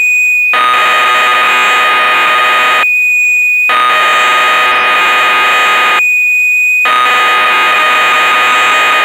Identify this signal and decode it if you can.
Cassette data (Kansas City type).
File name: magazines / Acorn User / 1985-09 Electron Programs